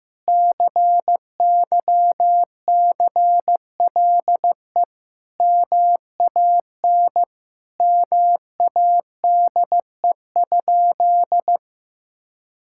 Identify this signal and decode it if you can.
Morse code, heard as a keyed tone, about 16 wpm, 700 Hz. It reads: CYCLE MAN MADE?
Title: Sentences from Top 1000 Words - Review Part 6 - 15wpm